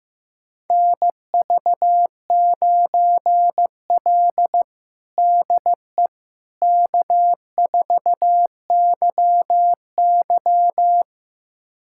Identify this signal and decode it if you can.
Morse code, heard as a keyed tone, about 16 wpm, 700 Hz. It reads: NV9L DE K4YY
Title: USA Callsign DE Callsign - 4 Characters - 15wpm